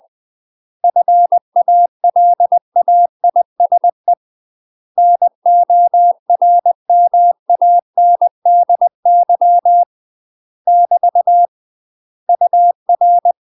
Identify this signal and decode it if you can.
Morse code, heard as a keyed tone, about 20 wpm, 700 Hz. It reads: FALAISE NORMANDY <BT> UR
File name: UK MASTER QSOs TRAINING / 20WPM / UK MASTER QSOs 0030